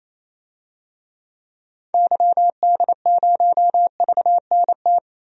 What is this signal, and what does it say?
Morse code, heard as a keyed tone, about 28 wpm, 700 Hz. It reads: YD0VNT
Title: DXCC-100 Call Signs - 6 Characters - 28wpm